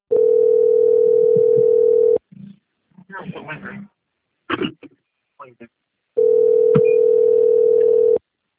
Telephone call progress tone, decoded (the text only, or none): ringback tone